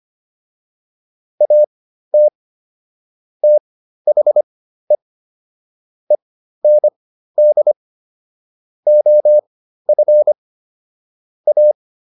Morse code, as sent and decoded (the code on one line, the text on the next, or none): .- - / - .... / . -. -.. / --- ..-. / .-
AT THE END OF A